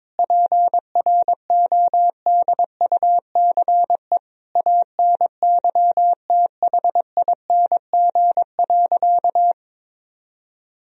Morse code, heard as a keyed tone, about 21 wpm, 700 Hz. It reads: PRODUCE ANYTHING.
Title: Sentences from Top 1000 Words - Review Part 3 - 22wpm